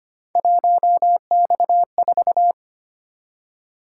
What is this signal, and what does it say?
Morse code, heard as a keyed tone, about 25 wpm, 700 Hz. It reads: 1X4
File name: Three Letter-Number Combinations - 25wpm